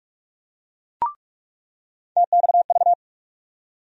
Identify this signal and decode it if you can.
Morse code, heard as a keyed tone, about 44 wpm, 700 Hz. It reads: TXV
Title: USA Call Sign Postfixes - 3 Letters 45wpm